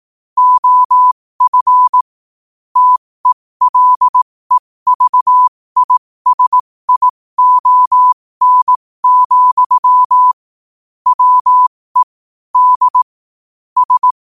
Morse code, heard as a keyed tone, about 18 wpm, spaced as for 15 wpm, 1000 Hz. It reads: OF TELEVISION, WE D S
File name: qotdfiles / 15wpm